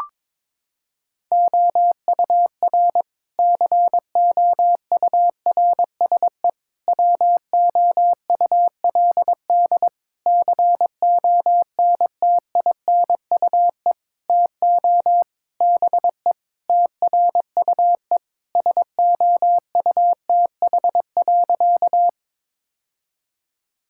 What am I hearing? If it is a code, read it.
Morse code, heard as a keyed tone, about 22 wpm, 700 Hz. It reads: OUR COURSE WOULD CONTINUE TO BE TRUE SOUTH.
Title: Sentences from Top 1000 Words - Review Part 6 - 22wpm